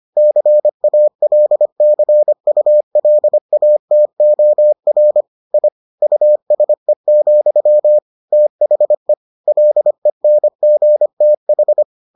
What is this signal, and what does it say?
Morse code, heard as a keyed tone, about 25 wpm, 600 Hz. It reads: CALCULATOR I USE, THE LENGTH